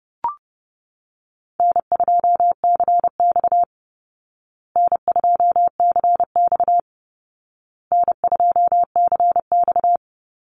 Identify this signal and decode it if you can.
Morse code, heard as a keyed tone, about 30 wpm, 700 Hz. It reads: N2CX N2CX N2CX
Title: Call Signs Repeated 2x - 30wpm